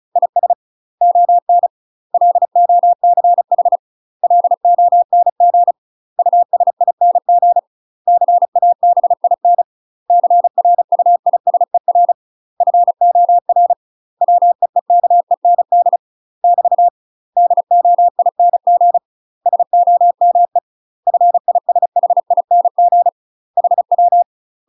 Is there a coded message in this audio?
Yes: Morse code from 0.2 to 24.2 s, heard as a keyed tone, about 36 wpm, 700 Hz. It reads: IS ON LOCH LONG USING CABIN CRUISER FOR WEEKEND <BT> DOING SOME FISHING HW